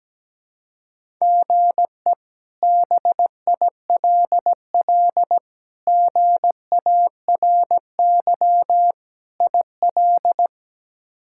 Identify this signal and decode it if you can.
Morse code, heard as a keyed tone, about 17 wpm, 700 Hz. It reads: GE BILL GARY IL
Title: K1USN SST QSO Practice - 17wpm